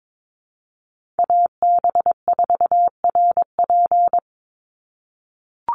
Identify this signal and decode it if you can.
Morse code, heard as a keyed tone, about 22 wpm, 700 Hz. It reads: AB4RP
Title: USA Call Signs 22wpm